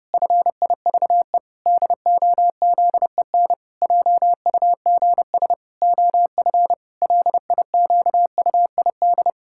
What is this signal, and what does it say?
Morse code, heard as a keyed tone, about 30 wpm, 700 Hz. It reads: FIVE DOZEN JUGS OF LIQUID
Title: Pangram Challenge v2 - 30wpm